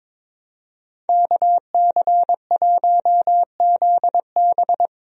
Morse code, heard as a keyed tone, about 22 wpm, 700 Hz. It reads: KC1ZB
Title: USA Call Signs - 5 Characters - 22wpm